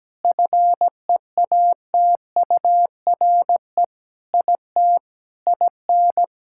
Morse code, heard as a keyed tone, about 17 wpm, 700 Hz. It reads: FEATURE IT IN
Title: Sentences from Top 1000 Words - Review Part 5 - 17wpm